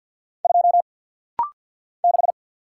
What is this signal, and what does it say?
Morse code, heard as a keyed tone, about 50 wpm, 690 Hz. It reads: 2 6
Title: Single Letter-Number - Rapid-Fire 50wpm